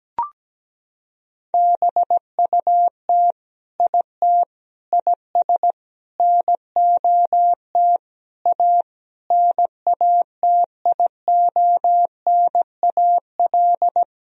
Morse code, heard as a keyed tone, about 17 wpm, 700 Hz. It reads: BUT IT IS NOT A NATIONAL
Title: Sentences from Top 1000 Words - Review Part 4 - 17wpm